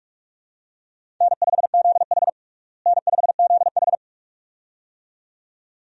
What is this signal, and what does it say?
Morse code, heard as a keyed tone, about 42 wpm, 700 Hz. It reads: N5ZH N5ZH
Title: USA Call Signs - 4 Characters - Repeated 2x - 45wpm